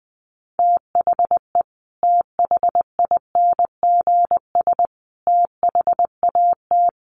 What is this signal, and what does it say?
Morse code, heard as a keyed tone, about 20 wpm, 700 Hz. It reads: THE THINGS THAT
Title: Sets of 3 Words 20wpm